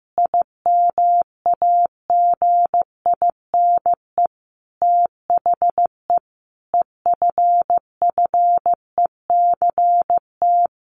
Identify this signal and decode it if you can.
Morse code, heard as a keyed tone, about 15 wpm, 700 Hz. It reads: IMAGINE THE EFFECT
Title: Sentences from Top 1400 Words 15wpm